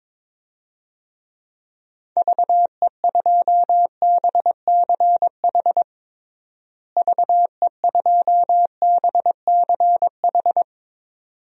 Morse code, heard as a keyed tone, about 22 wpm, 700 Hz. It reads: VE2BCH VE2BCH
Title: DXCC-100 Call Signs - 6 Characters - Repeated 2x - 22wpm